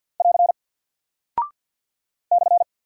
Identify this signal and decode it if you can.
Morse code, heard as a keyed tone, about 50 wpm, 700 Hz. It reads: <AR> /
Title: Single Character (Letter:Number:Punctuation:Prosign) 50wpm